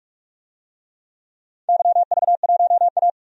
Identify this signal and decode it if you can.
Morse code, heard as a keyed tone, about 46 wpm, 700 Hz. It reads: YV1A
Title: DXCC-100 Call Signs - 4 Characters - 45wpm